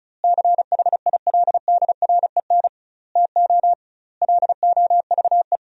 Morse code, heard as a keyed tone, about 36 wpm, 700 Hz. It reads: CHILDREN TO LOVE
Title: 2x Word-Spacing - Sentences from Top 400 Words - 35wpm